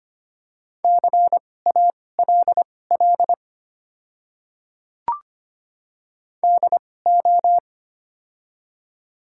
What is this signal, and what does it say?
Morse code, heard as a keyed tone, about 25 wpm, spaced as for 18 wpm, 690 Hz. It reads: CALL DO
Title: 2x Character Spacing - Top 100 Words - 25wpm